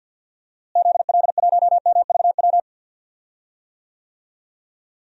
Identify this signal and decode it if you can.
Morse code, heard as a keyed tone, about 50 wpm, 700 Hz. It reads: ZL1MVW
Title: DXCC-100 Call Signs - 6 Characters - Repeated 3x - 50wpm